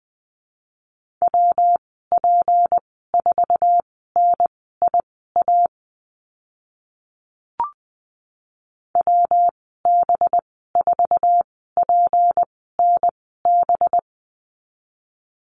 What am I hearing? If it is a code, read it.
Morse code, heard as a keyed tone, about 20 wpm, spaced as for 15 wpm, 700 Hz. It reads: WP4NIA WB4PNB
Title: 186 Lesson 30g - Letter b - Call Signs 20wpm